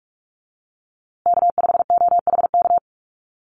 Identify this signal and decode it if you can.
Morse code, heard as a keyed tone, about 44 wpm, 700 Hz. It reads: K5OHK
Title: USA Call Signs Repeated 2x 45wpm